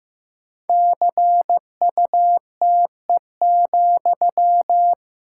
Morse code, heard as a keyed tone, about 15 wpm, 700 Hz. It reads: CUTE,
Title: Pangram Challenge v2 - 15wpm